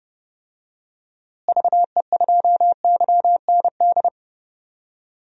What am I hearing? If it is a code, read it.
Morse code, heard as a keyed tone, about 30 wpm, 700 Hz. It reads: VE2YND
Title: DXCC-100 Call Signs - 6 Characters - Repeated 3x - 30wpm